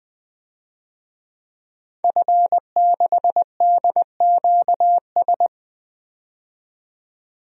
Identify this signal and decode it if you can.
Morse code, heard as a keyed tone, about 20 wpm, 700 Hz. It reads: F6DQS
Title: DXCC-100 Call Signs - 5 Characters - 20wpm